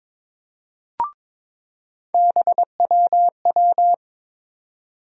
Morse code, heard as a keyed tone, about 22 wpm, 700 Hz. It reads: BWW
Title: USA Call Sign Postfixes - 3 Letters 22wpm